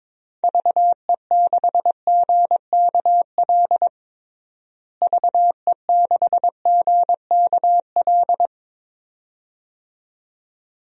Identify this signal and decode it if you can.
Morse code, heard as a keyed tone, about 22 wpm, 700 Hz. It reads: VE6GKL VE6GKL
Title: DXCC-100 Call Signs - 6 Characters - Repeated 2x - 22wpm